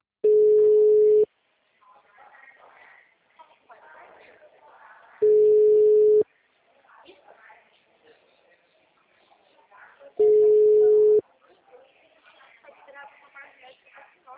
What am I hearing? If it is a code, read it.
Ringback tone (Europe, 425 Hz), with speech.